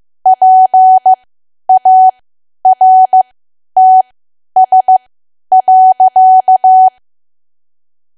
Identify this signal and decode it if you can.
Morse code, heard as a keyed tone, about 15 wpm, spaced as for 10 wpm, 750 Hz. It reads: PARTS.
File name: cwa / W1AW-Level2 / 10WPM